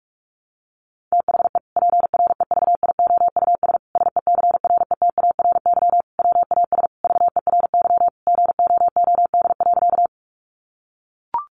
Morse code, heard as a keyed tone, about 44 wpm, 700 Hz. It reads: THE PREVIOUS SECRETARY WAS VERY GOOD.
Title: Sentences from Top 1500 Words 45wpm